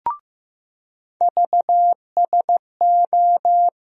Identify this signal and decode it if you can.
Morse code, heard as a keyed tone, about 15 wpm, 700 Hz. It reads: VSO
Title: USA Call Sign Postfixes - 3 Letters 15wpm